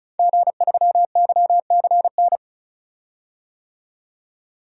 Morse code, heard as a keyed tone, about 36 wpm, 700 Hz. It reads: G3YCN
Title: DXCC-100 Call Signs - 5 Characters - 35wpm